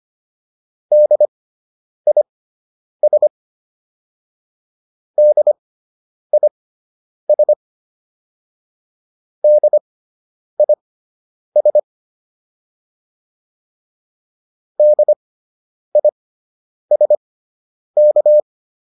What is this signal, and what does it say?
Morse code, heard as a keyed tone, about 25 wpm, spaced as for 9 wpm, 600 Hz. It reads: DIS DIS DIS DISK